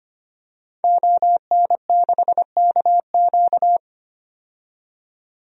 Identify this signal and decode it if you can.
Morse code, heard as a keyed tone, about 25 wpm, 700 Hz. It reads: ON6KQ
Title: DXCC-100 Call Signs - 5 Characters - 25wpm